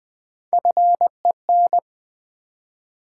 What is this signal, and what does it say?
Morse code, heard as a keyed tone, about 20 wpm, 700 Hz. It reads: FEN
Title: Build A Word - 1 Letter At A Time - 3 to 5-Letter Words - 20wpm